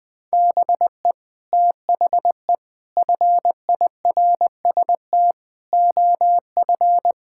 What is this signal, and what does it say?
Morse code, heard as a keyed tone, about 20 wpm, 700 Hz. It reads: BE THE FIRST OF
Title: Sets of 8 Words - 20wpm